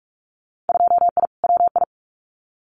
Morse code, heard as a keyed tone, about 44 wpm, 700 Hz. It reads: 2I WI
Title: Winter Field Day Exchanges 45wpm